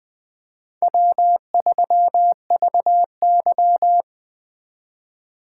Morse code, heard as a keyed tone, about 21 wpm, 700 Hz. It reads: W3VY
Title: Speed Racing - USA Call Signs 17wpm